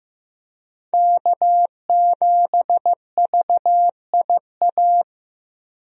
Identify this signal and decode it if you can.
Morse code, heard as a keyed tone, about 15 wpm, 700 Hz. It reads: K7VIA
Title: USA Call Signs - 5 Characters - Repeated 2x - 15wpm